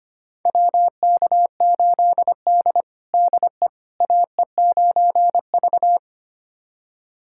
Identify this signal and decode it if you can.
Morse code, heard as a keyed tone, about 25 wpm, 700 Hz. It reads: WK8D DE AE9V
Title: USA Callsign DE Callsign - 4 Characters - 25wpm